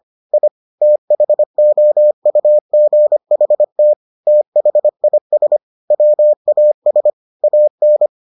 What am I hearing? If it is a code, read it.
Morse code, heard as a keyed tone, about 25 wpm, 600 Hz. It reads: I THOUGHT THIS WAS AN